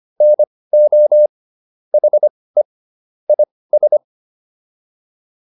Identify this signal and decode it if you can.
Morse code, heard as a keyed tone, about 25 wpm, spaced as for 18 wpm, 600 Hz. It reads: NO HE IS
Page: intermediate-practice-files